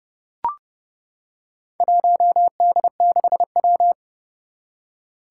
Morse code, heard as a keyed tone, about 30 wpm, 700 Hz. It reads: 1D6W
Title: Four Challenging Letter-Number Combinations 30wpm